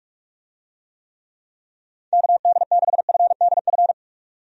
Morse code, heard as a keyed tone, about 46 wpm, 700 Hz. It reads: KD6FDF
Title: USA Call Signs - 6 Characters - 45wpm